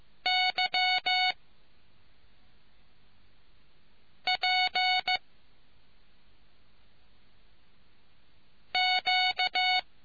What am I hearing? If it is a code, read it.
Morse code, heard as a keyed tone, about 15 wpm, 740 Hz. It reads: Y P Q